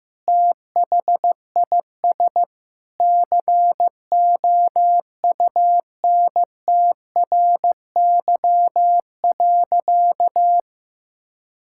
Morse code, heard as a keyed tone, about 15 wpm, 700 Hz. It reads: THIS COUNTRY.
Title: Sentences from Top 1000 Words - Review Part 5 - 15wpm